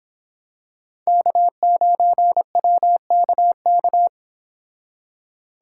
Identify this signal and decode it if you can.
Morse code, heard as a keyed tone, about 26 wpm, 700 Hz. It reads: K9WKK
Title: Speed Racing - USA Call Signs 22wpm